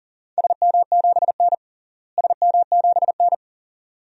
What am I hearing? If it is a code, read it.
Morse code, heard as a keyed tone, about 40 wpm, 700 Hz. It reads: SM7N SM7N
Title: DXCC-100 Call Signs - 4 Characters - Repeated 2x - 40wpm